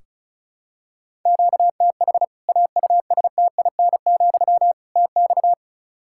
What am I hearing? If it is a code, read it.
Morse code, heard as a keyed tone, about 36 wpm, 700 Hz. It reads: QTH AUSTIN, TX
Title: Practice QSO - Exchange 1 - Beginning - 35wpm